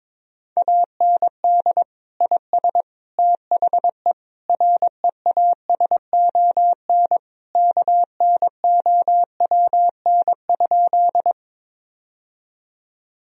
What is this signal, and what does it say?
Morse code, heard as a keyed tone, about 22 wpm, 700 Hz. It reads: AND IS THE REASON KNOWN?
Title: Sentences from Top 1000 Words - Review Part 3 - 22wpm